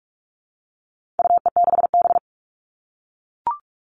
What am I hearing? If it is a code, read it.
Morse code, heard as a keyed tone, about 44 wpm, 700 Hz. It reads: UE6B
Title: All Cut Numbers - 4-Digit 45wpm